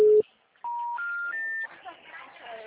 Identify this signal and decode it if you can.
Special information tone, with speech.